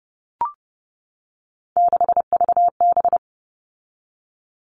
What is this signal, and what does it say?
Morse code, heard as a keyed tone, about 30 wpm, 700 Hz. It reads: 6VB
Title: All Cut Numbers - 3-Digit 30wpm